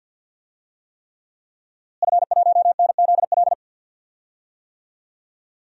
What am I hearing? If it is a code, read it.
Morse code, heard as a keyed tone, about 47 wpm, 700 Hz. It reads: F1NZL
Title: DXCC-100 Call Signs - 5 Characters - Repeated 3x - 50wpm